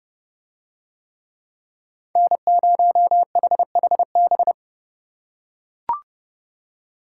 Morse code, heard as a keyed tone, about 29 wpm, 700 Hz. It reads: N0HHB